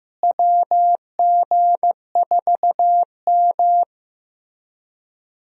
Morse code, heard as a keyed tone, about 16 wpm, 700 Hz. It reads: WG4M
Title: USA Call Signs - 4 Characters - Repeated 3x - 15wpm